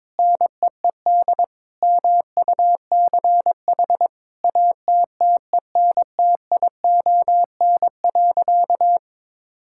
Morse code, heard as a keyed tone, about 22 wpm, 700 Hz. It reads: NEED MUCH ATTENTION.
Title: Sentences from Top 1000 Words - Review Part 6 - 22wpm